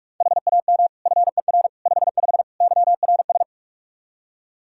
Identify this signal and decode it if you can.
Morse code, heard as a keyed tone, about 46 wpm, 700 Hz. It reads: HAM FER 55 YRS
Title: Practice QSO - Exchange 3 - More Info - 45wpm